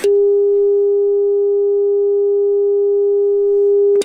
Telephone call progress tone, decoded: dial tone